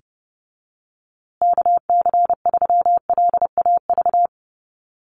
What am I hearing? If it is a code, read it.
Morse code, heard as a keyed tone, about 30 wpm, 700 Hz. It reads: KC3LAV
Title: USA Call Signs Repeated 3x 30wpm